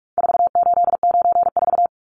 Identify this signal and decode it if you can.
Morse code, heard as a keyed tone, about 44 wpm, 700 Hz. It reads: 4894
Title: Numbers - 4-Digit 45wpm